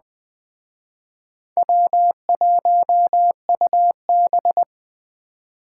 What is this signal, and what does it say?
Morse code, heard as a keyed tone, about 21 wpm, 700 Hz. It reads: W1UB